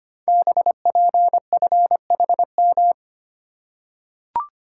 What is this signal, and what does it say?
Morse code, heard as a keyed tone, about 25 wpm, 700 Hz. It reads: BPFHM
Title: Code Groups - 5 Letters - 25wpm